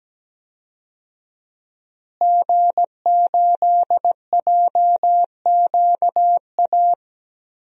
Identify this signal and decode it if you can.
Morse code, heard as a keyed tone, about 17 wpm, 700 Hz. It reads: G8JQA
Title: DXCC-100 Call Signs - 5 Characters - Repeated 3x - 17wpm